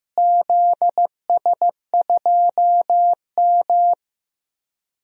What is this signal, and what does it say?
Morse code, heard as a keyed tone, about 15 wpm, 700 Hz. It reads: ZS2M
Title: DXCC-100 Call Signs - 4 Characters - Repeated 3x - 15wpm